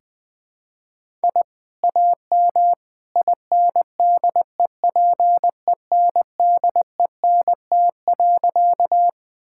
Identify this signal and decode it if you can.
Morse code, heard as a keyed tone, about 20 wpm, 700 Hz. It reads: I AM INDEPENDENT.
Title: Sentences from Top 1000 Words - Review Part 6 - 20wpm